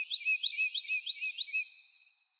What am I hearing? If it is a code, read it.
There is no code.